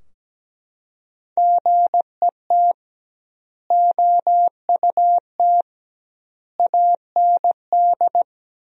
Morse code, heard as a keyed tone, about 17 wpm, 700 Hz. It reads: GET OUT AND
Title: 2x Word-Spacing - Sentences from Top 200 Words - 17wpm